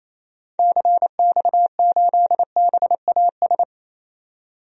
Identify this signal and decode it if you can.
Morse code, heard as a keyed tone, about 28 wpm, 700 Hz. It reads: CX8BAS
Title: DXCC-100 Call Signs - 6 Characters - 28wpm